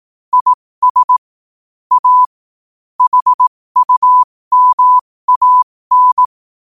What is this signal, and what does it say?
Morse code, heard as a keyed tone, about 18 wpm, spaced as for 15 wpm, 1000 Hz. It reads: IS A HUMAN
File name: qotdfiles / 15wpm